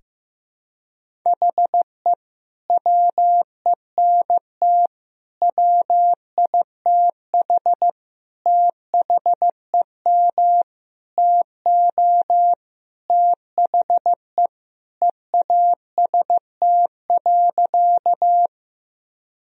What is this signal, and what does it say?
Morse code, heard as a keyed tone, about 15 wpm, 700 Hz. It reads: HE WENT WITH THEM TO THE EAST.
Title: Sentences from Top 1000 Words - Review Part 3 - 15wpm